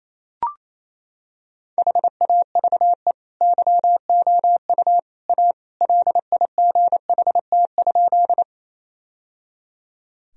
Morse code, heard as a keyed tone, about 28 wpm, 700 Hz. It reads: HAVE YOU A LIGHT?
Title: Sentences from Top 1000 Words - Review Part 1 - 28wpm